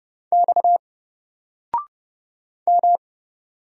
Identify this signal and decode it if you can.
Morse code, heard as a keyed tone, about 30 wpm, 700 Hz. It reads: X M
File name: Single-Letter-Number-30wpm